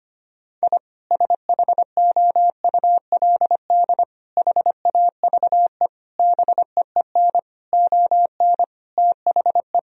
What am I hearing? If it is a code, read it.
Morse code, heard as a keyed tone, about 25 wpm, 700 Hz. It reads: I SHOULD HAVE BEEN ON THE
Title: Sentences from Top 1000 Words - Review Part 1 - 25wpm